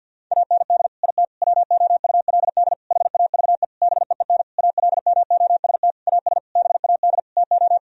Morse code, heard as a keyed tone, about 50 wpm, 700 Hz. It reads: AND IT WOULD HAVE BEEN ALMOST AS BAD TO
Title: Sentences from Top 1000 Words - Review Part 4 - 50wpm